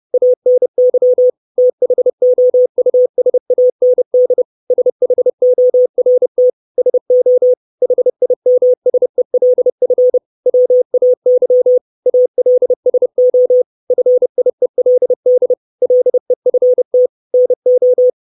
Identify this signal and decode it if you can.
Morse code, heard as a keyed tone, about 30 wpm, 500 Hz. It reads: ANY THOUSAND SHORT SO HIMSELF WAY ALSO FIELD LEFT NO
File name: 632 Words 30WPM